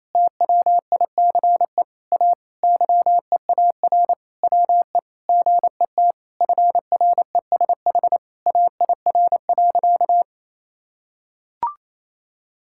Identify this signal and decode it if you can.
Morse code, heard as a keyed tone, about 27 wpm, 700 Hz. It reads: TWICE A YEAR WE GET FRESH AIR.
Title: Sentences from Top 1600 Words 28wpm